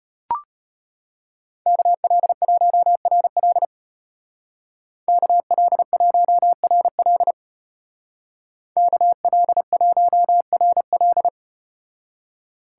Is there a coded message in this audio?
Yes: Morse code from 1.7 to 11.3 s, heard as a keyed tone, about 35 wpm, 700 Hz. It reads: KL1RL KL1RL KL1RL